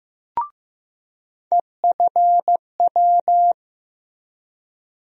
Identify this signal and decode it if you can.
Morse code, heard as a keyed tone, about 15 wpm, 700 Hz. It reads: EFW
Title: USA Call Sign Postfixes - 3 Letters 15wpm